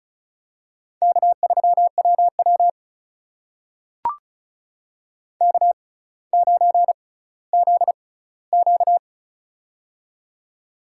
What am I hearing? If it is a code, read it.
Morse code, heard as a keyed tone, about 36 wpm, spaced as for 18 wpm, 700 Hz. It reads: K3WW K9ZQ
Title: USA Call Signs - 4 Characters - 6x Character Spacing - 35wpm